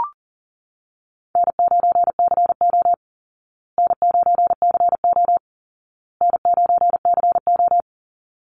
Morse code, heard as a keyed tone, about 40 wpm, 700 Hz. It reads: N9CO N9CO N9CO
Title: Easy Call Signs Repeated 3x - 40wpm